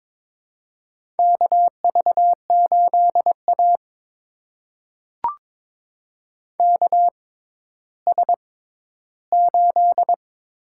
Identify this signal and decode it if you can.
Morse code, heard as a keyed tone, about 22 wpm, spaced as for 13 wpm, 700 Hz. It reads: KV8A KS8
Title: USA Call Signs - 4 Characters - 6x Character Spacing - 22wpm